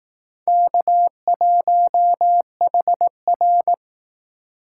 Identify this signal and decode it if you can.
Morse code, heard as a keyed tone, about 18 wpm, 700 Hz. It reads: K1HR
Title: Speed Racing - USA 1x2 Call Signs 15wpm